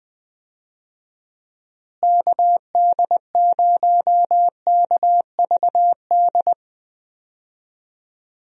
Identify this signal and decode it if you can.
Morse code, heard as a keyed tone, about 20 wpm, 700 Hz. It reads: KD0KVD
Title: USA Call Signs - 6 Characters - Repeated 2x - 20wpm